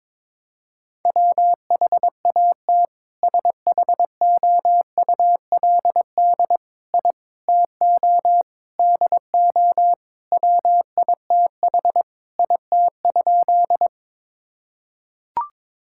Morse code, heard as a keyed tone, about 22 wpm, 700 Hz. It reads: WHAT SHOULD I TO DO WITH IT?
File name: practice / 22wpm / Sentences with 4-Word Phrases - 22wpm